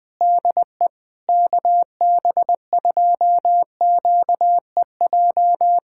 Morse code, heard as a keyed tone, about 20 wpm, 700 Hz. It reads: DE KB2QEJ
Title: USA Callsign DE Callsign 20wpm